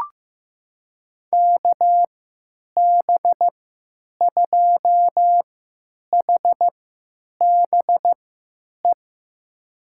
Morse code, heard as a keyed tone, about 15 wpm, spaced as for 8 wpm, 700 Hz. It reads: KB2HBE
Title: USA Call Signs - 6 Characters - 3x Character Spacing - 15wpm